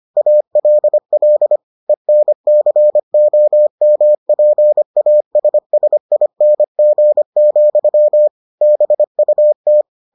Morse code, heard as a keyed tone, about 25 wpm, 600 Hz. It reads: ALL ENCOMPASSING, BUT